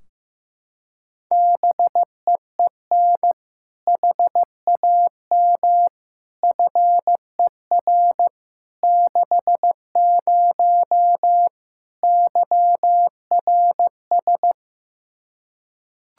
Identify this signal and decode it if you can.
Morse code, heard as a keyed tone, about 15 wpm, 700 Hz. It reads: BEEN HAM FER 60 YRS
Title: K1USN SST QSO Practice 15wpm